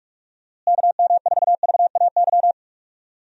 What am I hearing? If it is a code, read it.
Morse code, heard as a keyed tone, about 41 wpm, 700 Hz. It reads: KM4VAY
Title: USA Call Signs - 6 Characters - Repeated 2x - 45wpm